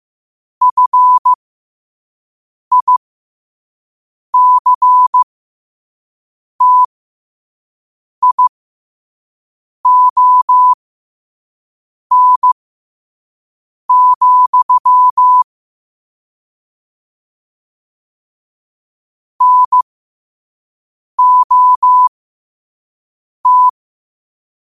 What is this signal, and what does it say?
Morse code, heard as a keyed tone, about 15 wpm, spaced as for 5 wpm, 1000 Hz. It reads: FICTION, NOT
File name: qotdfiles / 5wpm